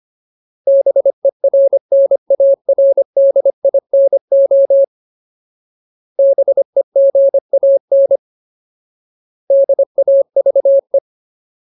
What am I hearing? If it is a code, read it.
Morse code, heard as a keyed tone, about 25 wpm, 550 Hz. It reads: BERNARDINO BEGAN DAVE
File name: Ultimate Word List / 25WPM Ultimate